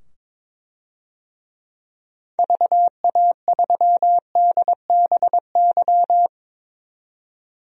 Morse code, heard as a keyed tone, about 22 wpm, 700 Hz. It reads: VA3DBY
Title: DXCC-100 Call Signs - 6 Characters - 22wpm